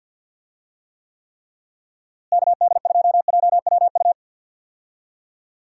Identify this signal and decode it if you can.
Morse code, heard as a keyed tone, about 46 wpm, 700 Hz. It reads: KD2JWU
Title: USA Call Signs - 6 Characters - Repeated 3x - 50wpm